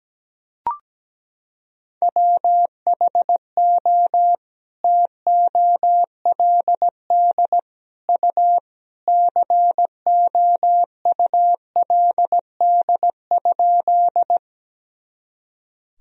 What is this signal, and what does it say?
Morse code, heard as a keyed tone, about 17 wpm, 700 Hz. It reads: WHO TOLD U COULD?
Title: Sentences from Top 400 Words and Abbreviations 17wpm